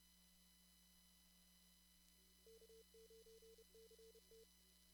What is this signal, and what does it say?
Morse code, heard as a keyed tone, about 30 wpm, 450 Hz, with noise. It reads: K9CT
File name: CW / kt5j wpx cw-op-k5pi